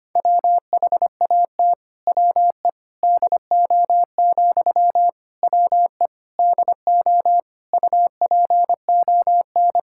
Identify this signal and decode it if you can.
Morse code, heard as a keyed tone, about 25 wpm, 700 Hz. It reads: WHAT WE DO, WE DO UPON
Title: Sentences from Top 1600 Words 25wpm